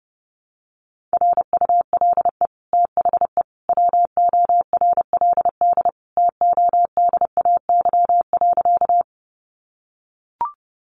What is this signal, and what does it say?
Morse code, heard as a keyed tone, about 30 wpm, 700 Hz. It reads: RULE THE WORLD TODAY.
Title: Sentences from Top 600 Words 30wpm